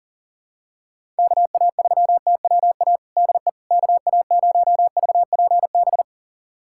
Morse code, heard as a keyed tone, about 40 wpm, 700 Hz. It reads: KA3TWA DE KA0VPB
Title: USA Callsign DE Callsign - 6 Characters - 40wpm